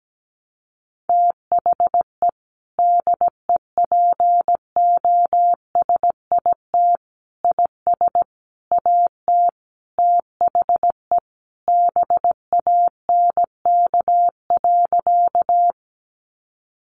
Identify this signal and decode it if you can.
Morse code, heard as a keyed tone, about 17 wpm, 700 Hz. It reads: THE DEPOSIT IS AT THE BANK.
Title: Sentences from Top 1300 Words 17wpm